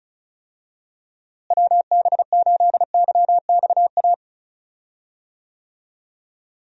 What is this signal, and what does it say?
Morse code, heard as a keyed tone, about 34 wpm, 700 Hz. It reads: WB8YXA
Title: USA Call Signs - 6 Characters - Repeated 2x - 35wpm